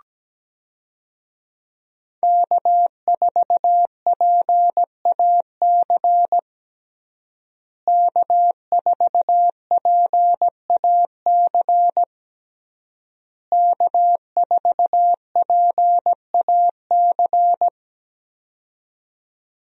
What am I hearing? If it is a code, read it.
Morse code, heard as a keyed tone, about 17 wpm, 700 Hz. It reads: K4PAC K4PAC K4PAC
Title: USA Call Signs - 5 Characters - Repeated 3x - 17wpm